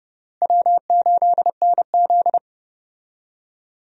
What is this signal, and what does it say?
Morse code, heard as a keyed tone, about 32 wpm, 700 Hz. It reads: W8NZ
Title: USA Call Signs - 4 Characters - Repeated 3x - 30wpm